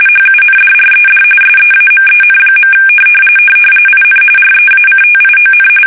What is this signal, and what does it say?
RTTY, about 45.45 baud. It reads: HUNDRED SEAMEN. IN REACTION
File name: fsk850 / usnavy1